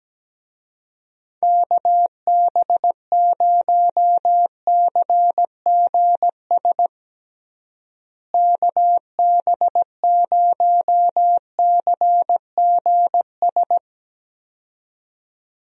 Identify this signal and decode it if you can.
Morse code, heard as a keyed tone, about 17 wpm, 700 Hz. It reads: KB0CGS KB0CGS
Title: USA Call Signs - 6 Characters - Repeated 2x - 17wpm